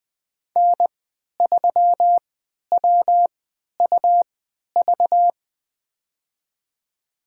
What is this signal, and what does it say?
Morse code, heard as a keyed tone, about 20 wpm, spaced as for 11 wpm, 700 Hz. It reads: N3WUV
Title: USA Call Signs - 5 Characters - 3x Character Spacing - 20wpm